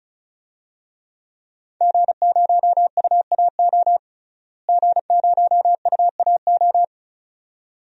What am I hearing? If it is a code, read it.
Morse code, heard as a keyed tone, about 34 wpm, 700 Hz. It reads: G0UAO G0UAO
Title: DXCC-100 Call Signs - 5 Characters - Repeated 2x - 35wpm